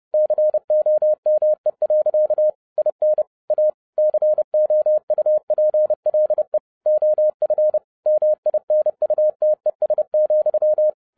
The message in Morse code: COME. IN A COUPLE OF MINUTES,

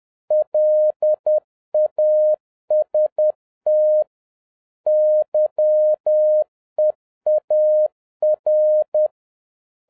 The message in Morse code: LAST YEAR

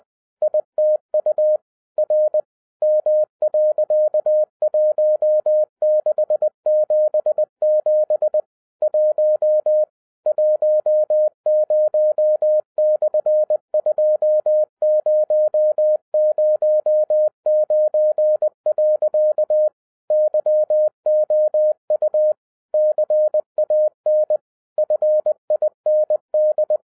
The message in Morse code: ITU R M.1677 1 10/2009. YOU CAN FIND

